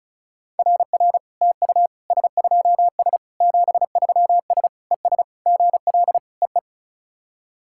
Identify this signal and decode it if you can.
Morse code, heard as a keyed tone, about 36 wpm, 690 Hz. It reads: RR TU S2S 73S ES GL EE
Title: SOTA QSO Practice 35wpm